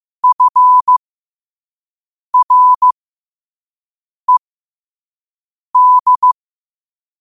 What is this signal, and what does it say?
Morse code, heard as a keyed tone, about 15 wpm, spaced as for 5 wpm, 1000 Hz. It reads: FRED
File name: qotdfiles / 5wpm